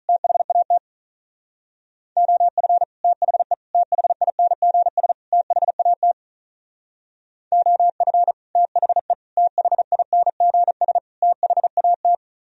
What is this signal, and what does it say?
Morse code, heard as a keyed tone, about 44 wpm, 700 Hz. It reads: THAT OF THE THINGS THAT OF THE THINGS THAT